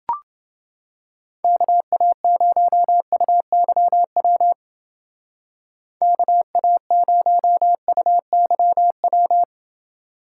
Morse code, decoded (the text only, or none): KA0UYW KA0UYW